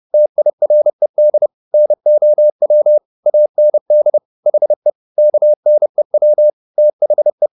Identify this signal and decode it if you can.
Morse code, heard as a keyed tone, about 30 wpm, 600 Hz. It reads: TIRED NOW AND HE KNEW THE
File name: the old man and the sea / the old man and the sea 1 30wpm